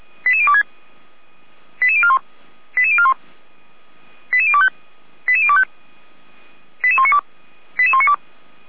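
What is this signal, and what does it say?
Five-tone selective call (ZVEI), heard as quick runs of five tones, tones of about 70 ms. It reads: 88026 88041 88041 88025 88025 88192 88192
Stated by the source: Five-tone ZVEI selective call received on SDR, logged by the Feuerwehr Orlamünde fire brigade (f85360) audio server.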